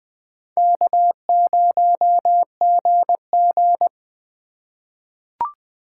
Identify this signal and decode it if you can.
Morse code, heard as a keyed tone, about 20 wpm, 700 Hz. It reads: K0GG
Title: USA Call Signs - 4 Characters - 3x Character Spacing - 20wpm